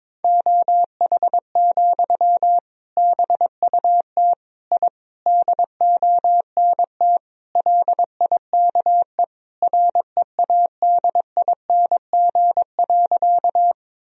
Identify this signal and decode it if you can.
Morse code, heard as a keyed tone, about 22 wpm, 700 Hz. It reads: OH, BUT I DONT LIKE READING.
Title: Sentences from Top 1000 Words - Review Part 3 - 22wpm